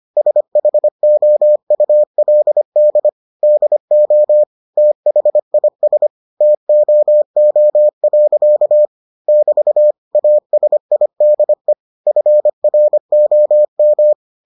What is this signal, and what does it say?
Morse code, heard as a keyed tone, about 25 wpm, 600 Hz. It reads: SHOULD DO THIS TOO. <BT> ASIDE FROM